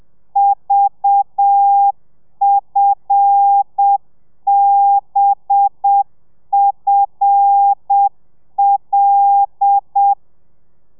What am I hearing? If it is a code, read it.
Morse code, heard as a keyed tone, about 7 wpm, 800 Hz. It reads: VFBFL
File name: telegrafia / i2tfj / Lez5 30